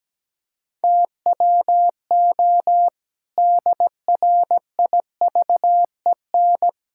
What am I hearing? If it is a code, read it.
Morse code, heard as a keyed tone, about 17 wpm, 700 Hz. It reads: TWO DRIVEN
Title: Pangram Challenge v2 - 17wpm